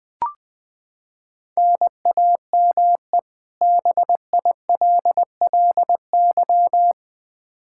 Morse code, heard as a keyed tone, about 20 wpm, 700 Hz. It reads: NAME BILLY
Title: Practice QSO - Exchange 1 - Beginning - 20wpm